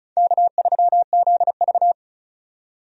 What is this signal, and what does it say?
Morse code, heard as a keyed tone, about 36 wpm, 700 Hz. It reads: K3ZV
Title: USA Call Signs - 4 Characters - 35wpm